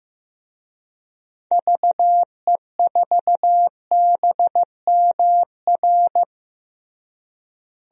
Morse code, heard as a keyed tone, about 15 wpm, 700 Hz. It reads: VE4BMR
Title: DXCC-100 Call Signs - 6 Characters - Repeated 3x - 15wpm